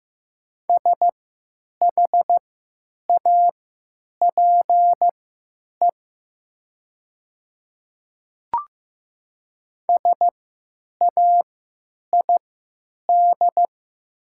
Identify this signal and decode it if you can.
Morse code, heard as a keyed tone, about 15 wpm, spaced as for 8 wpm, 700 Hz. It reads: SHAPE SAID
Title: 3x Character Spacing - Top 500 Words - 15wpm